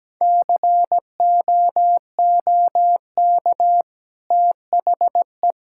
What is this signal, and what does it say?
Morse code, heard as a keyed tone, about 17 wpm, 700 Hz. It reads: COOK THE
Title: Sentences from Top 1000 Words - Review Part 6 - 17wpm